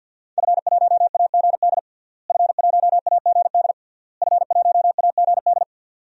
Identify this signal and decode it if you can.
Morse code, heard as a keyed tone, about 52 wpm, 700 Hz. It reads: F1AGD F1AGD F1AGD
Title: DXCC-100 Call Signs - 5 Characters - Repeated 3x - 50wpm